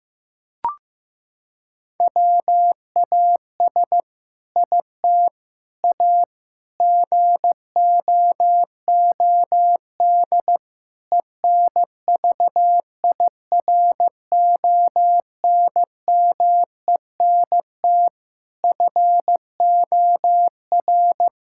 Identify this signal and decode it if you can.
Morse code, heard as a keyed tone, about 15 wpm, 700 Hz. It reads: WAS IT A GOOD ENVIRONMENT FOR
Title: Sentences from Top 1000 Words - Review Part 5 - 15wpm